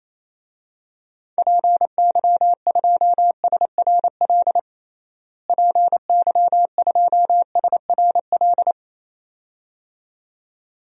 Morse code, heard as a keyed tone, about 28 wpm, 700 Hz. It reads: PY2SRL PY2SRL
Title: DXCC-100 Call Signs - 6 Characters - Repeated 2x - 28wpm